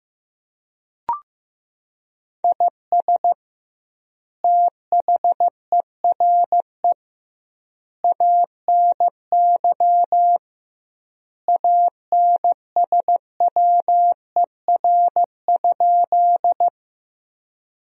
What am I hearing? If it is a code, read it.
Morse code, heard as a keyed tone, about 15 wpm, 700 Hz. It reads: IS THERE ANY ANSWER?
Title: 2x Word-Spacing - Sentences from Top 300 Words - 15wpm